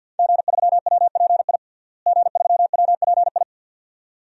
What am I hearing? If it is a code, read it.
Morse code, heard as a keyed tone, about 49 wpm, 700 Hz. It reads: G3WPI G3WPI
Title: DXCC-100 Call Signs - 5 Characters - Repeated 2x - 50wpm